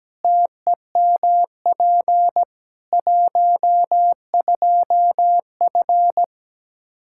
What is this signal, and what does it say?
Morse code, heard as a keyed tone, about 17 wpm, 700 Hz. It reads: TEMP 12F
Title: Practice QSO - Exchange 2 - Info - 17wpm